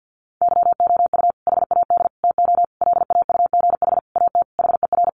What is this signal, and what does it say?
Morse code, heard as a keyed tone, about 50 wpm, 700 Hz. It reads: YOU HAD TO LAUGH AT HER
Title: Sentences from Top 600 Words 50wpm